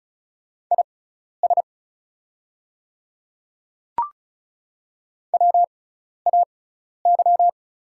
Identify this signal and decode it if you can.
Morse code, heard as a keyed tone, about 36 wpm, spaced as for 12 wpm, 700 Hz. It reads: IS WAY
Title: 6x Character Spacing - Top 100 Words - 35wpm